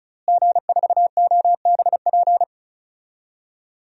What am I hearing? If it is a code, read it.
Morse code, heard as a keyed tone, about 34 wpm, 700 Hz. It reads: G4OBP
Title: DXCC-100 Call Signs - 5 Characters - Repeated 2x - 35wpm